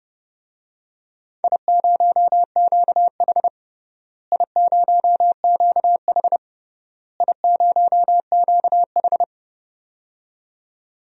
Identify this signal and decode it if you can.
Morse code, heard as a keyed tone, about 30 wpm, 700 Hz. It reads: I0QH I0QH I0QH
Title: DXCC-100 Call Signs - 4 Characters - Repeated 3x - 30wpm